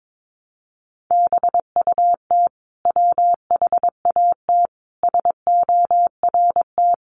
Morse code, heard as a keyed tone, about 22 wpm, 700 Hz. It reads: BUT WHAT SORT
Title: Sentences from Top 1300 Words 22wpm